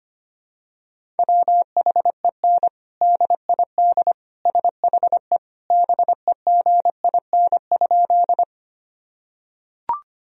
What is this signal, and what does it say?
Morse code, heard as a keyed tone, about 25 wpm, 700 Hz. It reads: WHEN DID SHE BEGIN?
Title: Sentences from Top 1000 Words - Review Part 1 - 25wpm